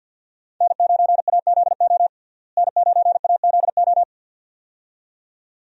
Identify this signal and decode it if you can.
Morse code, heard as a keyed tone, about 52 wpm, 700 Hz. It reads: N9AZO N9AZO
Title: USA Call Signs - 5 Characters - Repeated 2x - 50wpm